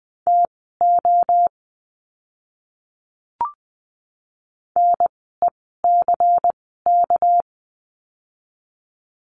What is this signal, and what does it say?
Morse code, heard as a keyed tone, about 20 wpm, spaced as for 14 wpm, 700 Hz. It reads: TO NECK
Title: 199 Lesson 32f - Letter k - Words 20wpm